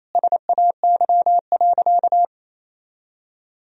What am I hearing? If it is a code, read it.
Morse code, heard as a keyed tone, about 28 wpm, 700 Hz. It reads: SAY.
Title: Sentences from Top 1000 Words - Review Part 2 - 28wpm